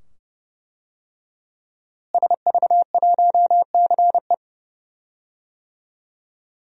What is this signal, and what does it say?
Morse code, heard as a keyed tone, about 32 wpm, 700 Hz. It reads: SV1CE
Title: DXCC-100 Call Signs - 5 Characters - Repeated 2x - 30wpm